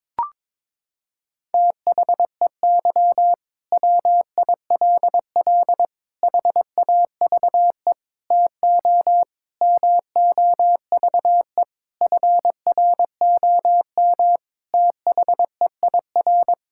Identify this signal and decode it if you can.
Morse code, heard as a keyed tone, about 22 wpm, 700 Hz. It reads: THEY WILL HAVE TO MOVE FROM THEIR
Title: Sentences from Top 1000 Words - Review Part 6 - 22wpm